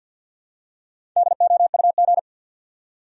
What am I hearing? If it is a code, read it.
Morse code, heard as a keyed tone, about 52 wpm, 700 Hz. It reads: DOUG
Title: USA Common Names - Max 5 Letters - 50wpm